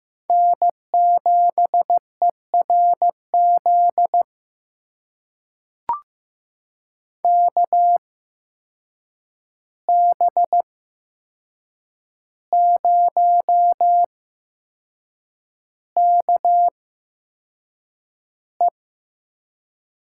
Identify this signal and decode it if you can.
Morse code, heard as a keyed tone, about 15 wpm, spaced as for 6 wpm, 700 Hz. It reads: N7ERZ KB0KE
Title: USA Call Signs - 5 Characters - 8x Character Spacing - 15wpm